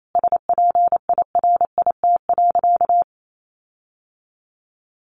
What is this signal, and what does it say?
Morse code, heard as a keyed tone, about 28 wpm, 700 Hz. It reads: SPIRIT.
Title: Sentences from Top 1400 Words 28wpm